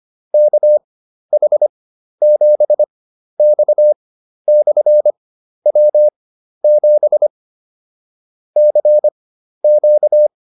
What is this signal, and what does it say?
Morse code, heard as a keyed tone, about 25 wpm, spaced as for 12 wpm, 600 Hz. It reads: KH7X/W7 CQ
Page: fundamental-practice-files